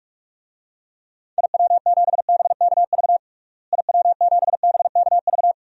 Morse code, heard as a keyed tone, about 46 wpm, 700 Hz. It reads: IW7BKV IW7BKV
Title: DXCC-100 Call Signs - 6 Characters - Repeated 3x - 45wpm